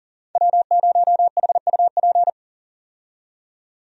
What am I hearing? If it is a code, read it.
Morse code, heard as a keyed tone, about 40 wpm, 700 Hz. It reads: W0HUP
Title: USA Call Signs - 5 Characters - 40wpm